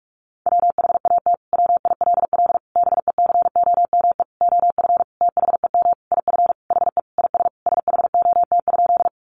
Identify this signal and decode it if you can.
Morse code, heard as a keyed tone, about 44 wpm, 700 Hz. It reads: WHAT WILL BECOME OF THEM IF HE IS SHOT?
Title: Sentences from Top 1100 Words 45wpm